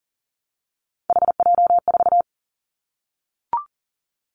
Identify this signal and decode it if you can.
Morse code, heard as a keyed tone, about 38 wpm, 690 Hz. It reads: HJ4
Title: Callsign Prefixes for Top 100 DXCC - 40wpm